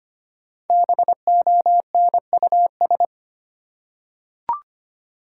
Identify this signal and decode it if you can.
Morse code, heard as a keyed tone, about 25 wpm, 700 Hz. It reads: BONUS
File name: 5 Letter Words - First 3 Letters Given - 25wpm